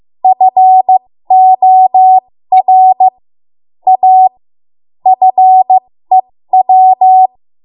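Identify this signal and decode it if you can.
Morse code, heard as a keyed tone, about 15 wpm, spaced as for 13 wpm, 750 Hz. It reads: FOR A FEW
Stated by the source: 13 wpm Morse code practice archive transmission.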